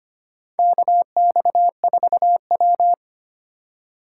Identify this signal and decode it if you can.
Morse code, heard as a keyed tone, about 25 wpm, 700 Hz. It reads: KX4W
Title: USA Call Signs - 4 Characters - 2x Character Spacing - 25wpm